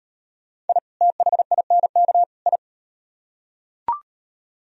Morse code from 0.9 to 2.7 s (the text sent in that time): THINK I